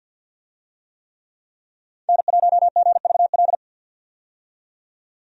Morse code, heard as a keyed tone, about 50 wpm, 700 Hz. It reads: N1GVL
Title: USA Call Signs - 5 Characters - 50wpm